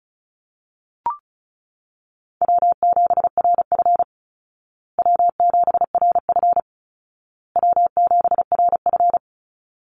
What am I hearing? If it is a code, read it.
Morse code, heard as a keyed tone, about 34 wpm, 700 Hz. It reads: W7RF W7RF W7RF
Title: Easy Call Signs Repeated 3x - 35wpm